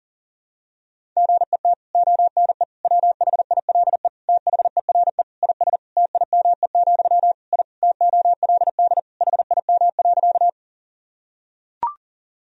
Morse code, heard as a keyed tone, about 40 wpm, 700 Hz. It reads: GET ONE WHILE THERE IS TIME, I TOLD HIM.
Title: Sentences from Top 1000 Words - Review Part 1 - 40wpm